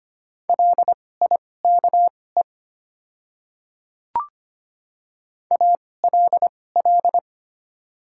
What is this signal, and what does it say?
Morse code, heard as a keyed tone, about 25 wpm, spaced as for 18 wpm, 690 Hz. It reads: LIKE ALL
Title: 2x Character Spacing - Top 100 Words - 25wpm